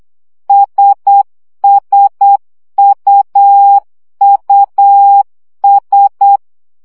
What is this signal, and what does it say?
Morse code, heard as a keyed tone, about 8 wpm, 800 Hz. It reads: SSUUS
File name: CW gyakorlas i s u 7 baud